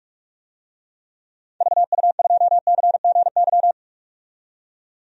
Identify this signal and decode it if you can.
Morse code, heard as a keyed tone, about 43 wpm, 700 Hz. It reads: VU2CGY
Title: DXCC-100 Call Signs - 6 Characters - 45wpm